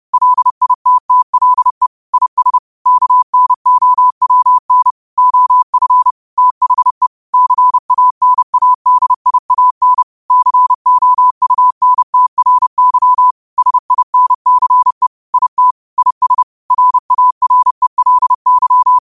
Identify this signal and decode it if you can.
Morse code, heard as a keyed tone, about 30 wpm, 1000 Hz. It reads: LITTLE IS KNOWN OF THE CANADIAN COUNTRY SINCE IT IS RARELY
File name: qotdfiles / 30wpm